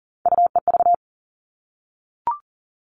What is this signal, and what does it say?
Morse code, heard as a keyed tone, about 40 wpm, 700 Hz. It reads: UEV
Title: All Cut Numbers - 3-Digit 40wpm